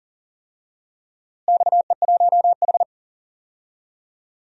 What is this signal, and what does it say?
Morse code, heard as a keyed tone, about 40 wpm, 700 Hz. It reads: XE1H
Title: DXCC-100 Call Signs - 4 Characters - Repeated 3x - 40wpm